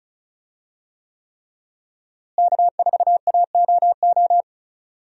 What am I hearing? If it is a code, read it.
Morse code, heard as a keyed tone, about 36 wpm, 700 Hz. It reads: K4AOO
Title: USA Call Signs - 5 Characters - Repeated 3x - 35wpm